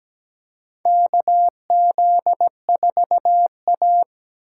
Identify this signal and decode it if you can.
Morse code, heard as a keyed tone, about 17 wpm, 700 Hz. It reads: KZ4A
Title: K1USN SST QSO Practice - 17wpm